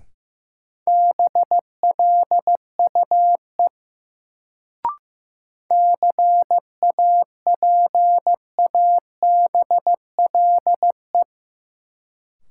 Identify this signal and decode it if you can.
Morse code, heard as a keyed tone, about 15 wpm, 700 Hz. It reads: BLUE CAPABLE
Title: Sentences from Top 1200 Words - New Word Drill Alt - 15wpm